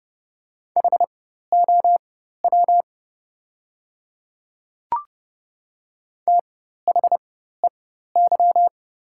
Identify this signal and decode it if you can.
Morse code, heard as a keyed tone, about 31 wpm, spaced as for 14 wpm, 700 Hz. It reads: HOW THEY
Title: 4x Character Spacing - Top 100 Words - 30wpm